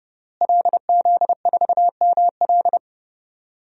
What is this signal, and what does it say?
Morse code, heard as a keyed tone, about 32 wpm, 700 Hz. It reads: LZ4ML
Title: DXCC-100 Call Signs - 5 Characters - Repeated 2x - 30wpm